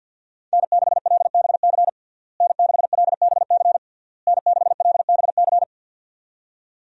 Morse code, heard as a keyed tone, about 48 wpm, 700 Hz. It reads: N6LBC N6LBC N6LBC
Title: USA Call Signs - 5 Characters - Repeated 3x - 50wpm